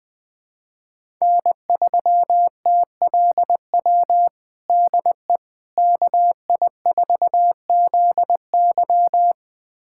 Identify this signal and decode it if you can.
Morse code, heard as a keyed tone, about 20 wpm, 700 Hz. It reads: N3TLW DE KI4ZY
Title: USA Callsign DE Callsign - 5 Characters - 20wpm